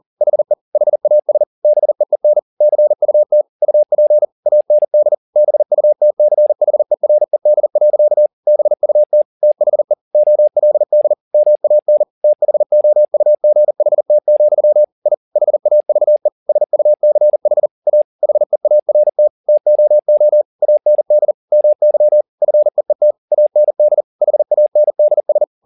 Morse code, heard as a keyed tone, about 40 wpm, 600 Hz. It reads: HE HAS BEEN CUT UP AND BUTCHERED. BUT THE OLD MAN THOUGHT, I HAVE SUCH A HEART TOO AND MY FEET AND HANDS